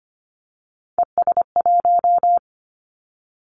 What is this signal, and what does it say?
Morse code, heard as a keyed tone, about 25 wpm, 700 Hz. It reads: ES1
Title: Callsign Prefixes for Top 100 DXCC - 25wpm